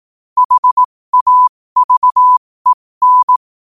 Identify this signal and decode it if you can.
Morse code, heard as a keyed tone, about 18 wpm, spaced as for 15 wpm, 1000 Hz. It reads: HAVEN
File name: qotdfiles / 15wpm